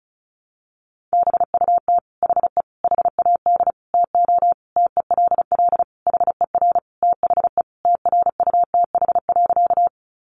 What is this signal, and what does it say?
Morse code, heard as a keyed tone, about 34 wpm, 700 Hz. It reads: BUT HE HAD TO TELL HER THE TRUTH.
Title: Sentences from Top 800 Words 35wpm